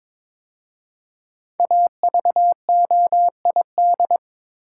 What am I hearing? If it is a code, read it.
Morse code, heard as a keyed tone, about 22 wpm, 700 Hz. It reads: AVOID
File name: 5 Letter Words - First 3 Letters Given - 22wpm